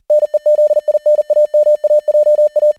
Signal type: Morse code